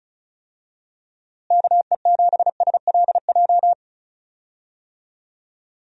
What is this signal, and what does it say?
Morse code, heard as a keyed tone, about 36 wpm, 700 Hz. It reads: KE7SLJ
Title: USA Call Signs - 6 Characters - Repeated 3x - 35wpm